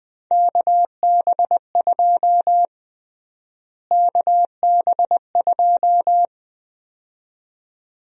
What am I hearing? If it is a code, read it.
Morse code, heard as a keyed tone, about 21 wpm, 700 Hz. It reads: KB2 KB2
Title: USA Call Sign Prefixes - Repeated 2x - 20wpm